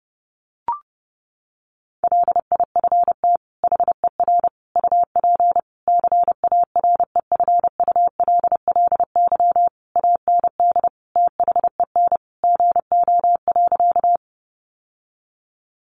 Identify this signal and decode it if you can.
Morse code, heard as a keyed tone, about 28 wpm, 700 Hz. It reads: LIFT HER UP CAREFULLY AND THEN GO.
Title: Sentences from Top 1300 Words 30wpm